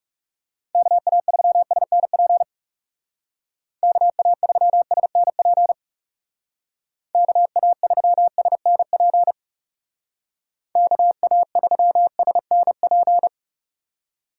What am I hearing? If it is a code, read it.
Morse code, heard as a keyed tone, about 37 wpm, 700 Hz. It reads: KA3SNP KA3SNP KA3SNP KA3SNP
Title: Speed Racing - USA 2x3 Call Signs 30wpm